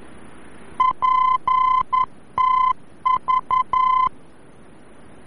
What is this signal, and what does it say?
Morse code, heard as a keyed tone, about 11 wpm, 1020 Hz. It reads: PTV
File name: france / mx / PTV-VOR-F